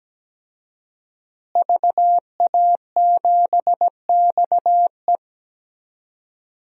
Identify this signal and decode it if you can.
Morse code, heard as a keyed tone, about 17 wpm, 700 Hz. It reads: VA7XE
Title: DXCC-100 Call Signs - 5 Characters - Repeated 3x - 17wpm